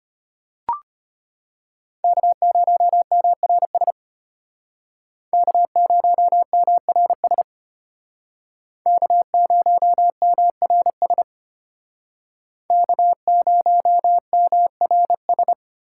Morse code, heard as a keyed tone, about 33 wpm, 700 Hz. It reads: K0MRS K0MRS K0MRS K0MRS